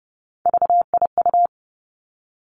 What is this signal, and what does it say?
Morse code, heard as a keyed tone, about 30 wpm, 700 Hz. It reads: VIU
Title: Three Challenging Letter-Number Combinations 30wpm